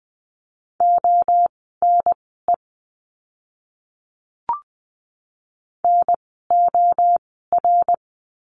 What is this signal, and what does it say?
Morse code, heard as a keyed tone, about 20 wpm, spaced as for 15 wpm, 700 Hz. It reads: ONE NOR
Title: 068 Lesson 12f - Letter d - Words 20wpm